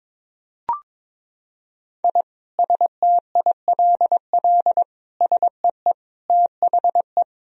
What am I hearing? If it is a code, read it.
Morse code, heard as a keyed tone, about 22 wpm, 700 Hz. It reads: I STILL SEE THE
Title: Sentences from Top 1000 Words - Review Part 1 - 22wpm